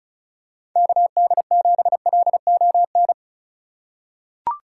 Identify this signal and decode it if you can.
Morse code, heard as a keyed tone, about 34 wpm, 690 Hz. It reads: KD7LON
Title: USA Call Signs - 6 Characters - 3x Character Spacing - 35wpm